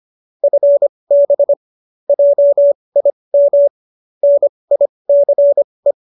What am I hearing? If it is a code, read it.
Morse code, heard as a keyed tone, about 25 wpm, spaced as for 20 wpm, 570 Hz. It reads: FB JIM NICE
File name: QSO307 20